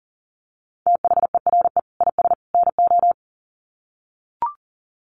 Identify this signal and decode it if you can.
Morse code, heard as a keyed tone, about 40 wpm, 700 Hz. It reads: THERE IS NO
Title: Sets of 3 Words 40wpm